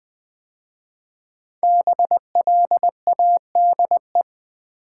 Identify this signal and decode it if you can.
Morse code, heard as a keyed tone, about 20 wpm, 700 Hz. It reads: BLADE
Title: 5 Letter Words - First 3 Letters Given - 20wpm